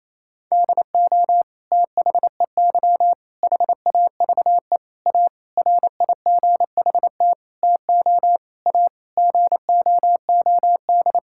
Morse code, heard as a keyed tone, about 27 wpm, 700 Hz. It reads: DO THEY HAVE A RIGHT TO A GOOD
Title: Sentences from Top 1000 Words - Review Part 6 - 28wpm